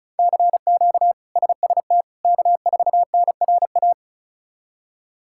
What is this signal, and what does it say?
Morse code, heard as a keyed tone, about 36 wpm, 700 Hz. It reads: CQ SST K4NRA
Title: K1USN SST QSO Practice - 35wpm